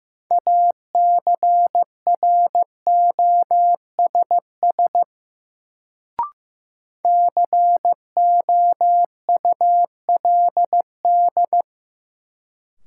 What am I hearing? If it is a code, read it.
Morse code, heard as a keyed tone, about 15 wpm, 700 Hz. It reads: ACROSS COULD
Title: Sets of 2 Words - New Word Drill - 15wpm